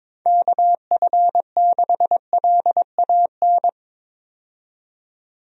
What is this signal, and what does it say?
Morse code, heard as a keyed tone, about 22 wpm, 700 Hz. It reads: KF6LAN
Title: USA Call Signs - 6 Characters - 22wpm